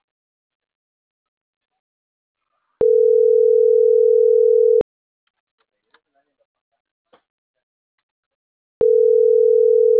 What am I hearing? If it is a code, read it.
Ringback tone (North America), 440 + 480 Hz.